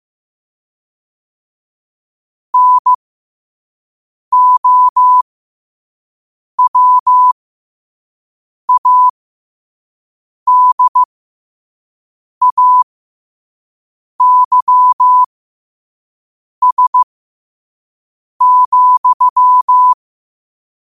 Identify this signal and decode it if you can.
Morse code, heard as a keyed tone, about 15 wpm, spaced as for 5 wpm, 1000 Hz. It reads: NOWADAYS,
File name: qotdfiles / 5wpm